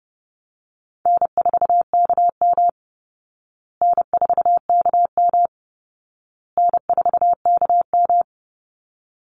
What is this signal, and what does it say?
Morse code, heard as a keyed tone, about 30 wpm, 700 Hz. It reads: N4KM N4KM N4KM
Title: Call Signs Repeated 2x - 30wpm